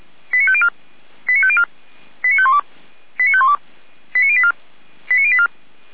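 Five-tone selective call (ZVEI), heard as quick runs of five tones, tones of about 70 ms. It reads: 89504 89504 89512 89512 89084 89084
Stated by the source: ZVEI five-tone call 89042 received by SDR.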